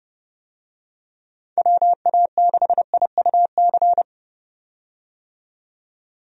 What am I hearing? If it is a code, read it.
Morse code, heard as a keyed tone, about 30 wpm, 700 Hz. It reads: WA6IUC
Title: USA Call Signs - 6 Characters - 30wpm